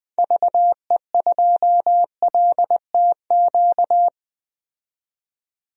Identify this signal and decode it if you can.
Morse code, heard as a keyed tone, about 20 wpm, 700 Hz. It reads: VE2LTQ
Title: DXCC-100 Call Signs - 6 Characters - 20wpm